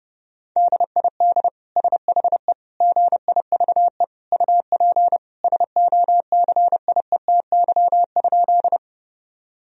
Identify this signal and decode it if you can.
Morse code, heard as a keyed tone, about 30 wpm, 700 Hz. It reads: DID SHE GIVE UP SOCIETY?
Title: Sentences from Top 1000 Words - Review Part 4 - 30wpm